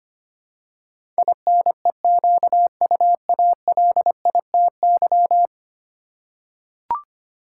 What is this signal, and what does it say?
Morse code, heard as a keyed tone, about 25 wpm, 700 Hz. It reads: INEQUALITY
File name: 10 Letter Words - First 3 Letters Given - 25wpm